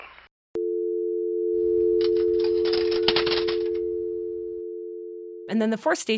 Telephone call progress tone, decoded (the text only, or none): dial tone